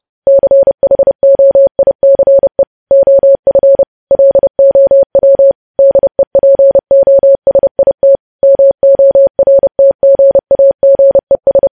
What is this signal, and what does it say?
Morse code, heard as a keyed tone, about 30 wpm, 550 Hz. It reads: CHOICE OF LOW DEPOSIT MORTGAGES
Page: news headlines in morse code at 30 wpm